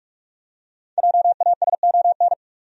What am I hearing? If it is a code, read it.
Morse code, heard as a keyed tone, about 47 wpm, 700 Hz. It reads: JASON
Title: USA Common Names - 45wpm